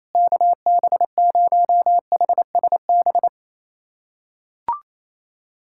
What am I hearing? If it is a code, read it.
Morse code, heard as a keyed tone, about 28 wpm, 700 Hz. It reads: KB0HSB
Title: USA Callsign DE Callsign - 6 Characters - 28wpm